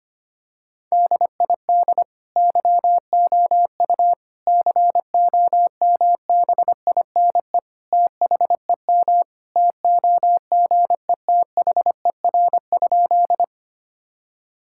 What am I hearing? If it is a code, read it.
Morse code, heard as a keyed tone, about 25 wpm, 700 Hz. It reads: DID YOU COMBINE THEM TOGETHER?
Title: Sentences from Top 1600 Words 25wpm